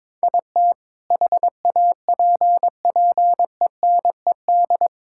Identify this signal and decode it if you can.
Morse code, heard as a keyed tone, about 22 wpm, 700 Hz. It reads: IT HAPPENED
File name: practice / 22wpm / Sentences with 2-Word Phrases - 22wpm